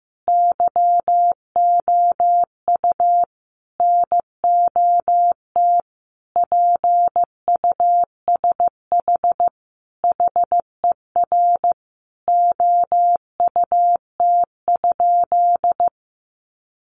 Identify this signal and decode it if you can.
Morse code, heard as a keyed tone, about 15 wpm, 700 Hz. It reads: YOU NOT PUSH HER OUT?
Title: Sentences from Top 1300 Words 15wpm